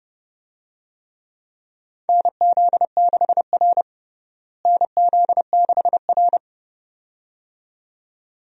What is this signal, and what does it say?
Morse code, heard as a keyed tone, about 30 wpm, 700 Hz. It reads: NZ6R NZ6R
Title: USA Call Signs - 4 Characters - Repeated 2x - 30wpm